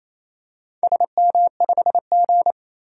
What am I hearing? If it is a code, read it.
Morse code, heard as a keyed tone, about 27 wpm, 700 Hz. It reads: SM5G
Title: DXCC-100 Call Signs - 4 Characters - 28wpm